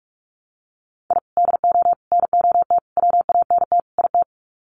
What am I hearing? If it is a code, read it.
Morse code, heard as a keyed tone, about 44 wpm, 700 Hz. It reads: I DO NOT WANT IT
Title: Sets of 5 Words Encore2 45wpm